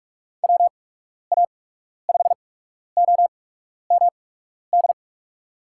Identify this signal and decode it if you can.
Morse code, heard as a keyed tone, about 46 wpm, spaced as for 12 wpm, 700 Hz. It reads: WA5OMD
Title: USA Call Signs - 6 Characters - 8x Character Spacing - 45wpm